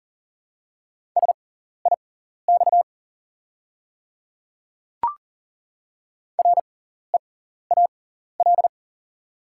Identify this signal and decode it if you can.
Morse code, heard as a keyed tone, about 40 wpm, spaced as for 14 wpm, 690 Hz. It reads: SIX REAL
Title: 6x Character Spacing - Top 500 Words - 40wpm